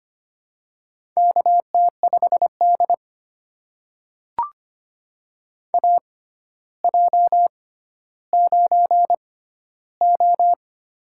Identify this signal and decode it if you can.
Morse code, heard as a keyed tone, about 25 wpm, spaced as for 13 wpm, 700 Hz. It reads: KT5D AJ9O
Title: USA Call Signs - 4 Characters - 6x Character Spacing - 25wpm